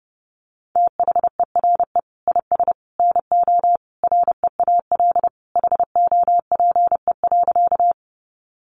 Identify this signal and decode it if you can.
Morse code, heard as a keyed tone, about 30 wpm, 700 Hz. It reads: THERE IS NO REAL HOPE.
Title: Sentences from Top 1000 Words 30wpm